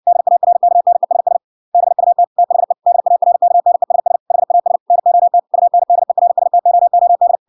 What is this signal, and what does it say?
Morse code, heard as a keyed tone, about 61 wpm, 700 Hz. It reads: DARKNESS BUT THE DARKNESS HAS NOT UNDERSTOOD